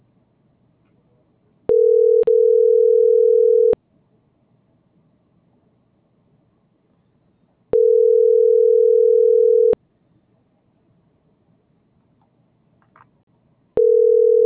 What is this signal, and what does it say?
Ringback tone (North America), 440 + 480 Hz.